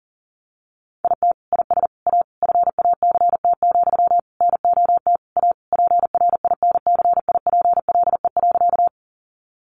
Morse code, heard as a keyed tone, about 40 wpm, 700 Hz. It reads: IT IS A FACT, NOT A PRINCIPLE.
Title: Sentences from Top 1400 Words 40wpm